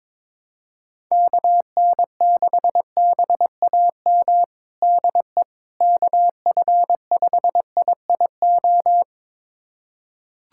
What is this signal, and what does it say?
Morse code, heard as a keyed tone, about 22 wpm, 700 Hz. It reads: KN6BAM DE KF5IIO
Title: USA Callsign DE Callsign - 6 Characters - 22wpm